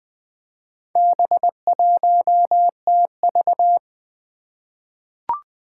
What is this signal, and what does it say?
Morse code, heard as a keyed tone, about 20 wpm, 700 Hz. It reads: B1TV
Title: Four Challenging Letter-Number Combinations 20wpm